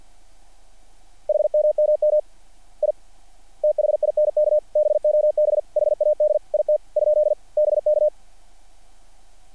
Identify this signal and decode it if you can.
Morse code, heard as a keyed tone, about 48 wpm, 660 Hz. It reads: HMMM I THINK BOB HAD IT ? BK
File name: archived / CRSnet